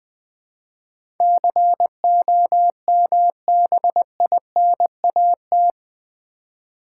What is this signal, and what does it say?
Morse code, heard as a keyed tone, about 21 wpm, 700 Hz. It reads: COMBINAT
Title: Build A Word - 4 Letters At A Time - 11-Letter Words - 20wpm